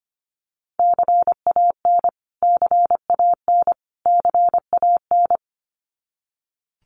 Morse code, heard as a keyed tone, about 25 wpm, 700 Hz. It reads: CAN CAN CAN
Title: Top 300 Words Repeated 25wpm